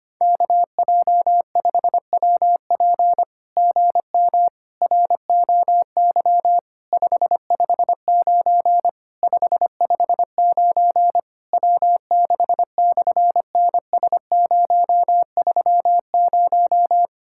Morse code, heard as a keyed tone, about 25 wpm, 700 Hz. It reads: KJ5WP GM ROY 559 559 W6/NS030
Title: SOTA QSO Practice 25wpm